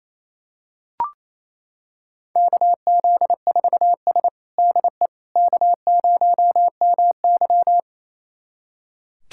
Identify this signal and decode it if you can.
Morse code, heard as a keyed tone, about 28 wpm, 700 Hz. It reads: KZ4S DE K0MY
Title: USA Callsign DE Callsign - 4 Characters - 28wpm